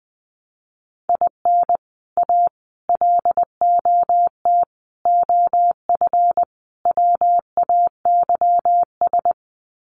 Morse code, heard as a keyed tone, about 20 wpm, 700 Hz. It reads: IN A LOT OF WAYS
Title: Sets of 5 Words Encore 20wpm